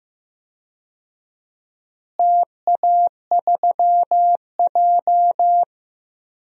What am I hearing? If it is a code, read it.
Morse code, heard as a keyed tone, about 15 wpm, 700 Hz. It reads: TA3J
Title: DXCC-100 Call Signs - 4 Characters - Repeated 3x - 15wpm